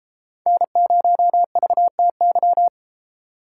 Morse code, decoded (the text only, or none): N0VTY